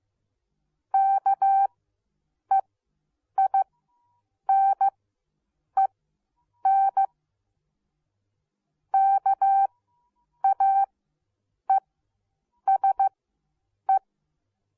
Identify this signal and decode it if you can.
Morse code, heard as a keyed tone, about 15 wpm, spaced as for 8 wpm, 780 Hz. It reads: KEINEN KAESE